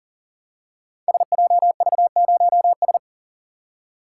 Morse code, heard as a keyed tone, about 41 wpm, 700 Hz. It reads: SJV0S
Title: Code Groups - 5 Letter:Numbers - 40wpm